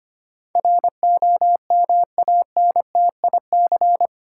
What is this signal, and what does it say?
Morse code, heard as a keyed tone, about 25 wpm, 690 Hz. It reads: ROMANTIC
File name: 8 Letter Words - First 3 Letters Given - 25wpm